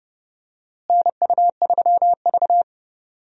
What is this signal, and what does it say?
Morse code, heard as a keyed tone, about 30 wpm, 700 Hz. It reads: NU3V
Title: USA Call Signs - 4 Characters - Repeated 3x - 30wpm